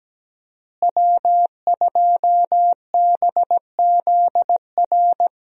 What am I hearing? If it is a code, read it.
Morse code, heard as a keyed tone, about 17 wpm, 700 Hz. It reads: W2BZR
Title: USA Call Signs - 5 Characters - Repeated 3x - 17wpm